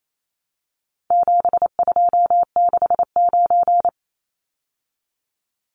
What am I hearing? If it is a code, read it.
Morse code, heard as a keyed tone, about 28 wpm, 700 Hz. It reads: 7269
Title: Numbers - 4-Digit 28wpm